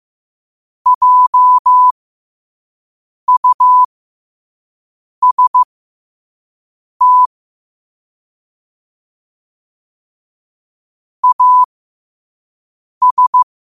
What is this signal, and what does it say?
Morse code, heard as a keyed tone, about 15 wpm, spaced as for 5 wpm, 1000 Hz. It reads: JUST AS
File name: qotdfiles / 5wpm